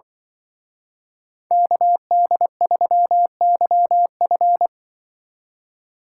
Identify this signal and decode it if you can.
Morse code, heard as a keyed tone, about 24 wpm, 700 Hz. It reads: KD3YF
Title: Speed Racing - USA 2x2 Call Signs 20wpm